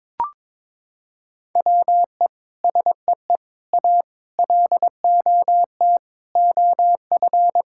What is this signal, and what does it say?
Morse code, heard as a keyed tone, about 22 wpm, 700 Hz. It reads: WE SEE A LOT OF
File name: Sentences with 9-Word Phrases - 22wpm